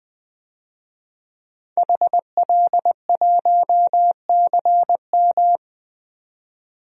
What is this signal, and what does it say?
Morse code, heard as a keyed tone, about 20 wpm, 700 Hz. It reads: HL1CM
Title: DXCC-100 Call Signs - 5 Characters - 20wpm